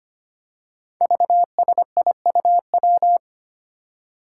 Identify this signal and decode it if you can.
Morse code, heard as a keyed tone, about 25 wpm, 700 Hz. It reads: VSIUW
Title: Five Challenging Letter-Number Combinations 25wpm